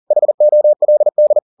Morse code, heard as a keyed tone, about 40 wpm, 600 Hz. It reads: HOLD
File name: the old man and the sea 2 40wpm